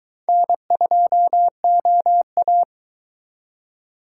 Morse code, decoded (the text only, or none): N2OA